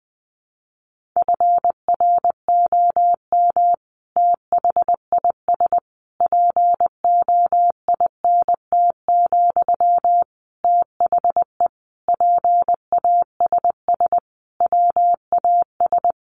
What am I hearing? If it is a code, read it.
Morse code, heard as a keyed tone, about 20 wpm, 700 Hz. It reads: FROM THIS POINT, THE PASS WAS
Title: Sentences from Top 1200 Words 20wpm